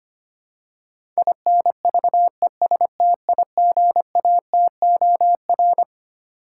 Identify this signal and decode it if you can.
Morse code, heard as a keyed tone, about 25 wpm, 700 Hz. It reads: INVESTIGATOR
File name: Twelve Letter Words - 25wpm